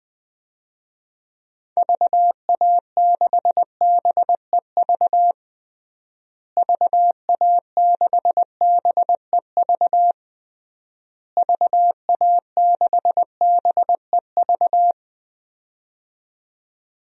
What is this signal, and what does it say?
Morse code, heard as a keyed tone, about 20 wpm, 700 Hz. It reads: VA6BEV VA6BEV VA6BEV
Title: DXCC-100 Call Signs - 6 Characters - Repeated 3x - 20wpm